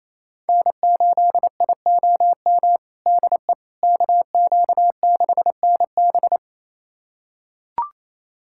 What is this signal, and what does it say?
Morse code, heard as a keyed tone, about 27 wpm, 700 Hz. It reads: N8IOM DE KQ6NB
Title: USA Callsign DE Callsign - 5 Characters - 28wpm